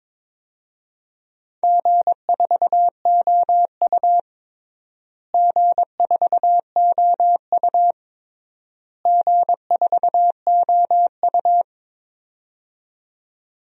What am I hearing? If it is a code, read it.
Morse code, heard as a keyed tone, about 22 wpm, 700 Hz. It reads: G4OU G4OU G4OU
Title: DXCC-100 Call Signs - 4 Characters - Repeated 3x - 22wpm